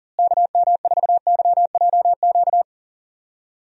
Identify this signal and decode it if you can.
Morse code, heard as a keyed tone, about 40 wpm, 700 Hz. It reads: KM4YJQ
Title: USA Call Signs - 6 Characters - Repeated 3x - 40wpm